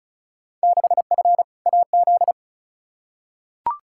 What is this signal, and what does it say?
Morse code, heard as a keyed tone, about 34 wpm, 700 Hz. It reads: 6F AZ
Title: Field Day Exchanges Encore - 35wpm